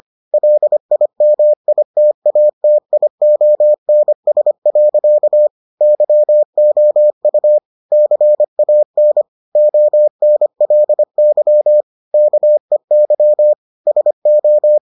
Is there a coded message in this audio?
Yes: Morse code from 0.3 to 14.8 s, heard as a keyed tone, about 25 wpm, 600 Hz. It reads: LIMITATIONS. YOU CAN ONLY KEY SO